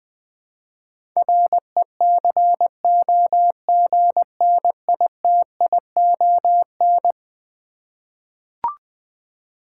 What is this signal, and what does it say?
Morse code, heard as a keyed tone, about 20 wpm, 700 Hz. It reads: RECOGNITION
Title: Eleven Letter Words - 20wpm